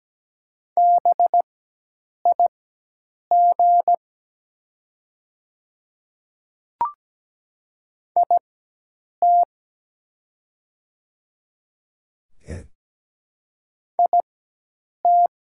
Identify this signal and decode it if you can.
Morse code, heard as a keyed tone, about 17 wpm, spaced as for 8 wpm, 700 Hz. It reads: BIG IT IT
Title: 4x Character Spacing - Top 500 Words - 17wpm